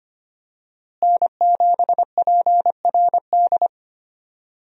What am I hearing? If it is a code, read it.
Morse code, heard as a keyed tone, about 25 wpm, 700 Hz. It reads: N7PRD
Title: USA Call Signs - 5 Characters - Repeated 2x - 25wpm